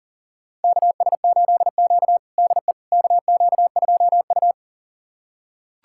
Morse code, heard as a keyed tone, about 40 wpm, 700 Hz. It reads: KS8Q DE KQ2U
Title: USA Callsign DE Callsign - 4 Characters - 40wpm